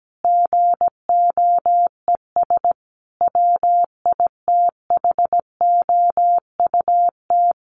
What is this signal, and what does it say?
Morse code, heard as a keyed tone, about 17 wpm, 700 Hz. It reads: GOES WITHOUT
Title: Sentences from Top 1500 Words 17wpm